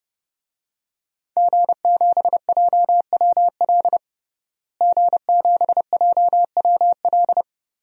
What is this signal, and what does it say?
Morse code, heard as a keyed tone, about 29 wpm, 700 Hz. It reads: G7JWL G7JWL
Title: DXCC-100 Call Signs - 5 Characters - Repeated 3x - 30wpm